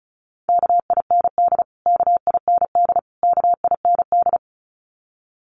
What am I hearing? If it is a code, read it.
Morse code, heard as a keyed tone, about 34 wpm, 700 Hz. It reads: KIND KIND KIND
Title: Top 500 Words Repeated 35wpm